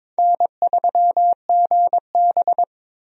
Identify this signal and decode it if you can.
Morse code, heard as a keyed tone, about 22 wpm, 700 Hz. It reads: N3GB
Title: USA Call Signs - 4 Characters - 22wpm